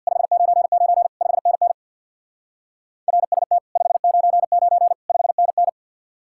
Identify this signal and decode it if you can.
Morse code, heard as a keyed tone, about 64 wpm, 700 Hz. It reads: 599 5NN RST 599 5NN